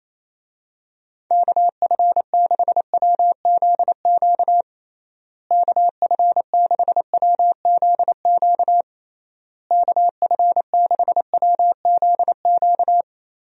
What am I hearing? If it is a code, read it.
Morse code, heard as a keyed tone, about 28 wpm, 700 Hz. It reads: KF6WZQ KF6WZQ KF6WZQ
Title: USA Call Signs - 6 Characters - Repeated 3x - 28wpm